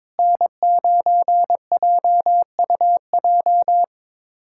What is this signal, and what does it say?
Morse code, heard as a keyed tone, about 21 wpm, 700 Hz. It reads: N9JUJ
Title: USA Callsign DE Callsign - 5 Characters - 22wpm